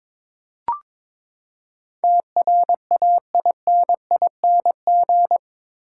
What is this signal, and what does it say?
Morse code, heard as a keyed tone, about 22 wpm, 700 Hz. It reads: TRAINING
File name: practice / 22wpm / Eight Letter Words - 22wpm